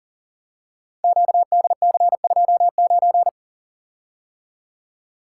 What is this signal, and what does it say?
Morse code, heard as a keyed tone, about 40 wpm, 700 Hz. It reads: QDC29
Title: Code Groups - 5 Letter:Numbers - 40wpm